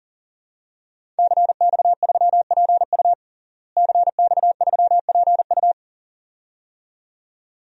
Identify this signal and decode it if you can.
Morse code, heard as a keyed tone, about 40 wpm, 700 Hz. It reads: CX3PU CX3PU
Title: DXCC-100 Call Signs - 5 Characters - Repeated 2x - 40wpm